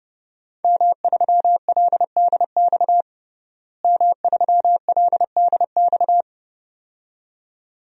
Morse code, heard as a keyed tone, about 32 wpm, 700 Hz. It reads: M3LDX M3LDX
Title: DXCC-100 Call Signs - 5 Characters - Repeated 2x - 30wpm